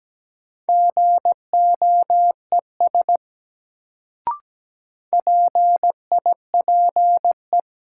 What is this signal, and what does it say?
Morse code, heard as a keyed tone, about 17 wpm, 700 Hz. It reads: GOES PIPE
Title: Sentences from Top 1500 Words - New Word Drill Alt - 17wpm